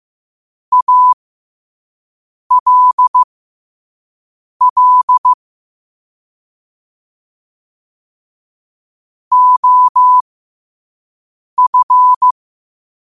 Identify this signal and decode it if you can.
Morse code, heard as a keyed tone, about 15 wpm, spaced as for 5 wpm, 1000 Hz. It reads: ALL OF
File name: qotdfiles / 5wpm